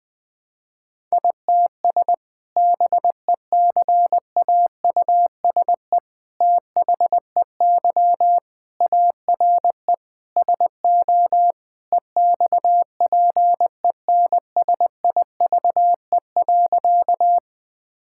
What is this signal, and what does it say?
Morse code, heard as a keyed tone, about 19 wpm, 700 Hz. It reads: ITS BECAUSE THEY ARE SO EXPENSIVE.
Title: Sentences from Top 1000 Words - Review Part 6 - 20wpm